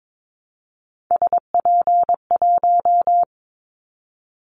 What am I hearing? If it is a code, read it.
Morse code, heard as a keyed tone, about 22 wpm, 700 Hz. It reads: SP1
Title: Callsign Prefixes for Top 25 DXCC - 22wpm